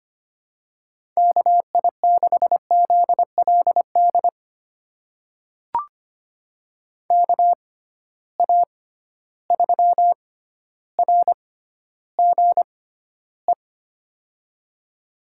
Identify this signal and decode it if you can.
Morse code, heard as a keyed tone, about 25 wpm, spaced as for 13 wpm, 700 Hz. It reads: KI6ZLD KA3RGE
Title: USA Call Signs - 6 Characters - 6x Character Spacing - 25wpm